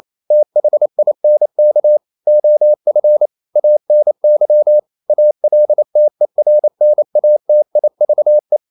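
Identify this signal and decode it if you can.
Morse code, heard as a keyed tone, about 28 wpm, 600 Hz. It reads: THINK OF ANY ALTERNATIVE
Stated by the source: Morse code practice at 28 wpm, text of Ken Follett's novel Never in English, from CW med Gnister.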